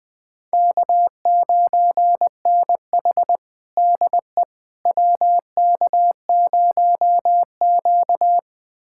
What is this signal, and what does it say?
Morse code, heard as a keyed tone, about 20 wpm, 700 Hz. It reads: K9NH DE WK0Q
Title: USA Callsign DE Callsign - 4 Characters - 20wpm